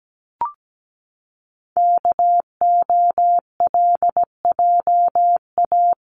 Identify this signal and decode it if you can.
Morse code, heard as a keyed tone, about 17 wpm, 700 Hz. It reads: KOLJA
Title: CWT Exchange 17wpm